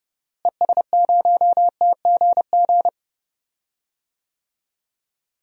Morse code, heard as a keyed tone, about 30 wpm, 700 Hz. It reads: ES0TGG
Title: DXCC-100 Call Signs - 6 Characters - Repeated 3x - 30wpm